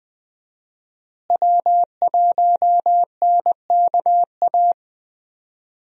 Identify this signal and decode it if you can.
Morse code, heard as a keyed tone, about 20 wpm, 700 Hz. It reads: W1NKA
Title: USA Call Signs - 5 Characters - Repeated 3x - 20wpm